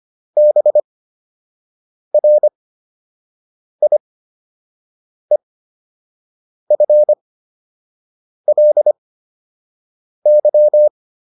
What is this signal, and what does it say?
Morse code, heard as a keyed tone, about 25 wpm, spaced as for 6 wpm, 600 Hz. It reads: BRIEFLY